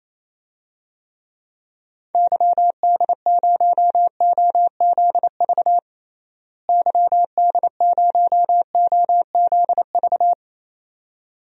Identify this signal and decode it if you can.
Morse code, heard as a keyed tone, about 28 wpm, 700 Hz. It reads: YD0OZV YD0OZV
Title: DXCC-100 Call Signs - 6 Characters - Repeated 2x - 28wpm